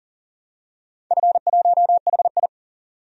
Morse code, heard as a keyed tone, about 40 wpm, 700 Hz. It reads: F1HI
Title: DXCC-100 Call Signs - 4 Characters - Repeated 3x - 40wpm